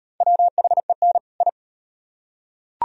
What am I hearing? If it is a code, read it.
Morse code, heard as a keyed tone, about 38 wpm, 700 Hz. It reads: WHEN I